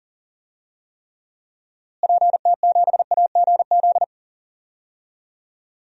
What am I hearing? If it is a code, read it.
Morse code, heard as a keyed tone, about 40 wpm, 700 Hz. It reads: PT7AGZ
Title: DXCC-100 Call Signs - 6 Characters - Repeated 3x - 40wpm